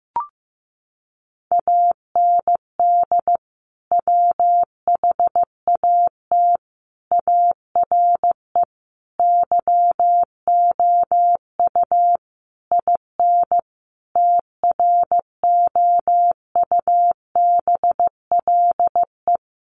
Morse code, heard as a keyed tone, about 15 wpm, 700 Hz. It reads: AND WHAT ARE YOU IN TROUBLE
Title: Sentences from Top 1200 Words 15wpm